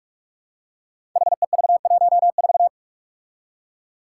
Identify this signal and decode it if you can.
Morse code, heard as a keyed tone, about 44 wpm, 700 Hz. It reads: HEV14
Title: Five Challenging Letter-Number Combinations 45wpm